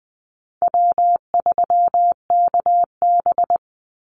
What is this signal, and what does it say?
Morse code, heard as a keyed tone, about 19 wpm, 700 Hz. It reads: W3KB
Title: Easy Call Signs Repeated 3x - 20wpm